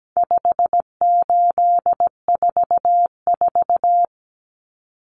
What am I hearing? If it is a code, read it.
Morse code, heard as a keyed tone, about 17 wpm, 700 Hz. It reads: 5844
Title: Common Cut Numbers - 4-Digit 17wpm